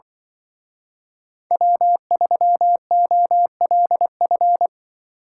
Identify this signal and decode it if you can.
Morse code, heard as a keyed tone, about 23 wpm, 700 Hz. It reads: W3OLF
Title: Speed Racing - USA 1x3 Call Signs 20wpm